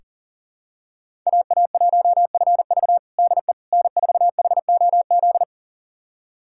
Morse code, heard as a keyed tone, about 40 wpm, 700 Hz. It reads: AA1FV DE N4HOZ
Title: USA Callsign DE Callsign - 5 Characters - 40wpm